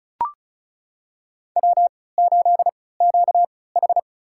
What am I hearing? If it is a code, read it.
Morse code, heard as a keyed tone, about 36 wpm, spaced as for 20 wpm, 700 Hz. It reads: W8QH
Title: USA Call Signs - 4 Characters - 3x Character Spacing - 35wpm